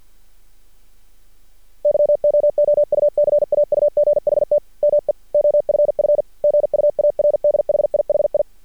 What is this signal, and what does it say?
Morse code, heard as a keyed tone, about 51 wpm, 590 Hz. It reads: YOOU CAUGHT ME OFF GUARD HIHI
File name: archived / CRSnet